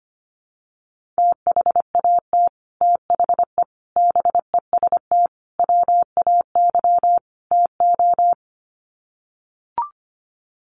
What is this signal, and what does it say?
Morse code, heard as a keyed tone, about 25 wpm, 700 Hz. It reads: THAT THE BEST WAY TO
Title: Sets of 5 Words Encore2 25wpm